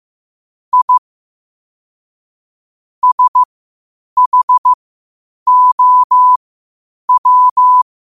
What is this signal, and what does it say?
Morse code, heard as a keyed tone, about 15 wpm, spaced as for 8 wpm, 1000 Hz. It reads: I SHOW
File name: qotdfiles / 8wpm